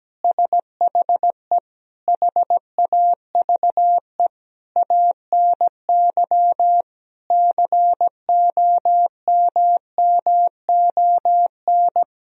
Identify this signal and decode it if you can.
Morse code, heard as a keyed tone, about 17 wpm, 700 Hz. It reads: SHE HAVE ANY COMMON
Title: Sentences from Top 1000 Words - Review Part 4 - 17wpm